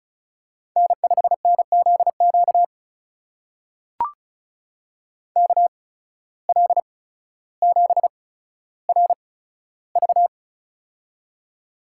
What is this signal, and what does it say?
Morse code, heard as a keyed tone, about 36 wpm, spaced as for 15 wpm, 700 Hz. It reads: N5NZQ KL7RV
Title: USA Call Signs - 5 Characters - 8x Character Spacing - 35wpm